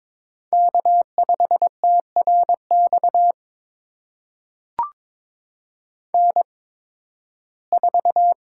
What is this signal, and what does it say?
Morse code, heard as a keyed tone, about 22 wpm, spaced as for 14 wpm, 700 Hz. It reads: K5TRX N4
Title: USA Call Signs - 5 Characters - 8x Character Spacing - 22wpm